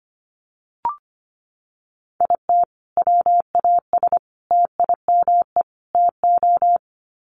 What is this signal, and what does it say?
Morse code, heard as a keyed tone, about 25 wpm, 700 Hz. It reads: IT WAS TIME TO
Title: Sets of 4 Words 25wpm